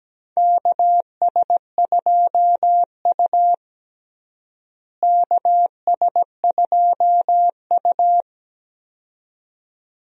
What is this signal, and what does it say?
Morse code, heard as a keyed tone, about 17 wpm, 700 Hz. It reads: KS2U KS2U
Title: USA Call Signs - 4 Characters - Repeated 2x - 17wpm